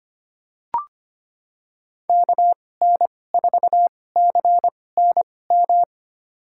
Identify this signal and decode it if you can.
Morse code, heard as a keyed tone, about 25 wpm, spaced as for 18 wpm, 700 Hz. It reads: KN4CNM
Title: USA Call Signs - 2x Character Spacing - 25wpm